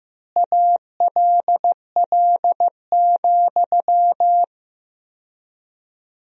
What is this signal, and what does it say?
Morse code, heard as a keyed tone, about 15 wpm, 700 Hz. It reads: ALL,
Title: 4x Word-Spacing - Sentences from Top 500 Words - 15wpm